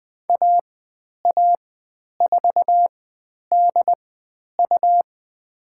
Morse code, heard as a keyed tone, about 20 wpm, spaced as for 10 wpm, 700 Hz. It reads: AA4DU
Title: 109 Lesson 18g - Letter m - Call Signs 20wpm